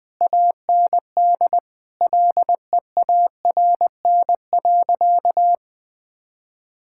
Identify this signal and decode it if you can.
Morse code, heard as a keyed tone, about 20 wpm, 700 Hz. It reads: AND LEARN.
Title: Difficult Sentences with Hints - 20wpm